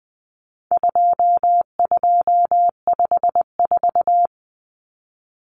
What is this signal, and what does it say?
Morse code, heard as a keyed tone, about 20 wpm, 700 Hz. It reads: 2254
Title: Common Cut Numbers - 4-Digit 20wpm